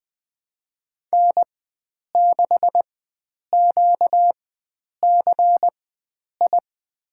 Morse code, heard as a keyed tone, about 20 wpm, spaced as for 9 wpm, 700 Hz. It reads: N6QCI
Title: USA Call Signs - 4x Character Spacing - 20wpm